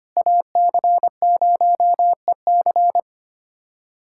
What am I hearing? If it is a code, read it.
Morse code, heard as a keyed tone, about 25 wpm, 700 Hz. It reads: AC0EC
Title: USA Callsign DE Callsign - 5 Characters - 25wpm